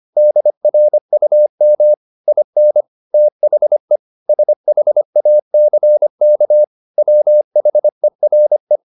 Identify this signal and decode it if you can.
Morse code, heard as a keyed tone, about 25 wpm, 600 Hz. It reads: DRUM IN THE SHACK WHERE
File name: the old man and the sea part 1 25wpm